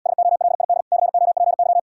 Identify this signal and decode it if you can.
Morse code, heard as a keyed tone, about 79 wpm, 700 Hz. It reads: SOLID COPY